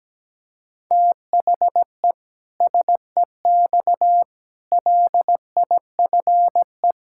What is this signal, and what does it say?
Morse code, heard as a keyed tone, about 17 wpm, 700 Hz. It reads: THE SEX LIFE
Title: Pangram Challenge v2 - 17wpm